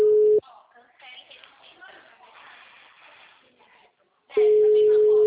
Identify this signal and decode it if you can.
Ringback tone (Europe, 425 Hz), with speech.